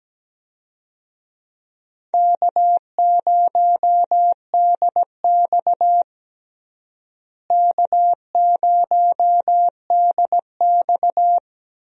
Morse code, heard as a keyed tone, about 17 wpm, 700 Hz. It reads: K0DX K0DX
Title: USA Call Signs - 4 Characters - Repeated 2x - 17wpm